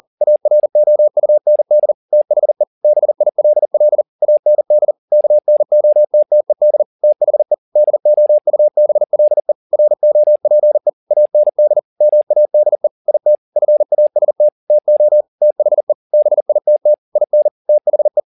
Morse code, heard as a keyed tone, about 40 wpm, 600 Hz. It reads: AROUND THE BILL AND KNOTTED THE DOUBLE ROPE AND MADE IT FAST TO THE BITT IN THE
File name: the old man and the sea 2 40wpm